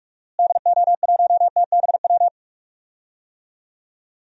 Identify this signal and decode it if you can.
Morse code, heard as a keyed tone, about 42 wpm, 690 Hz. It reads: DO1TBW
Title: DXCC-100 Call Signs - 6 Characters - Repeated 2x - 45wpm